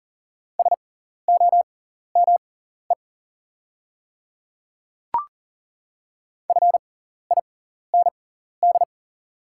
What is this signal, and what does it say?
Morse code, heard as a keyed tone, about 38 wpm, spaced as for 14 wpm, 690 Hz. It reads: SOME FIND
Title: 6x Character Spacing - Top 100 Words - 40wpm